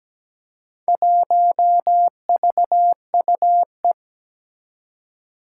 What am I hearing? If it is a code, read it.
Morse code, heard as a keyed tone, about 17 wpm, 700 Hz. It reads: 1VUE
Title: Four Challenging Letter-Number Combinations 17wpm